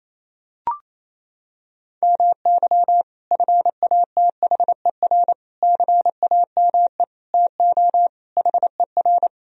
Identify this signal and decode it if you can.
Morse code, heard as a keyed tone, about 27 wpm, 700 Hz. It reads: MY FATHER CAME TO HER
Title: Sentences from Top 1000 Words - Review Part 1 - 28wpm